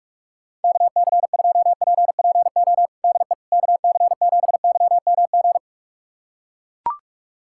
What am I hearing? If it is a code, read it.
Morse code, heard as a keyed tone, about 41 wpm, 700 Hz. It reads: KC2PPO DE KC7YMG
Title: USA Callsign DE Callsign - 6 Characters - 45wpm